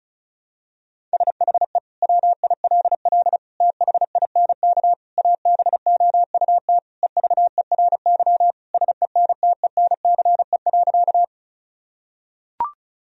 Morse code, heard as a keyed tone, about 36 wpm, 700 Hz. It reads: SHE WILL THINK ABOUT EVERY SENTENCE.
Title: Sentences from Top 1000 Words - Review Part 1 - 35wpm